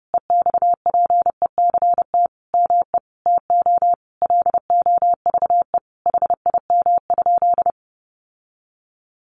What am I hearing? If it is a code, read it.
Morse code, heard as a keyed tone, about 30 wpm, 700 Hz. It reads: EXPECT ME TO LOVE HIM?
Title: Sentences from Top 1100 Words 30wpm